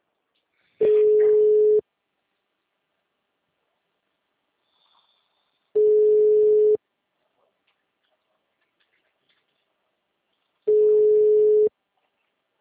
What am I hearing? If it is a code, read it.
Ringback tone (Europe, 425 Hz), with speech.
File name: out- -2035- - - .41009